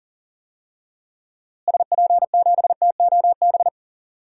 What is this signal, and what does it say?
Morse code, heard as a keyed tone, about 39 wpm, 700 Hz. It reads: SP7TOB
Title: DXCC-100 Call Signs - 6 Characters - Repeated 3x - 40wpm